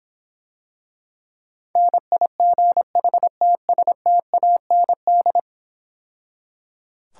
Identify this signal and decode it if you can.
Morse code, heard as a keyed tone, about 26 wpm, 700 Hz. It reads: NIGHTSTAND